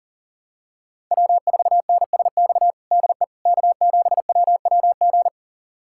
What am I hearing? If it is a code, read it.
Morse code, heard as a keyed tone, about 40 wpm, 700 Hz. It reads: W4NSX DE K7WWG
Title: USA Callsign DE Callsign 40wpm